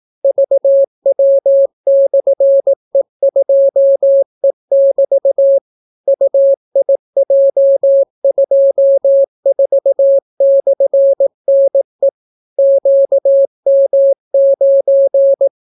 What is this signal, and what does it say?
Morse code, heard as a keyed tone, about 18 wpm, 550 Hz. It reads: VW/E2E<BT> UIJ24/NE QM9